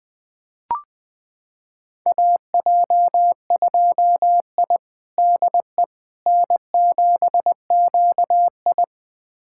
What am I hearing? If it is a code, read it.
Morse code, heard as a keyed tone, about 20 wpm, 700 Hz. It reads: AJ2I DE N7QI
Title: USA Callsign DE Callsign - 4 Characters - 20wpm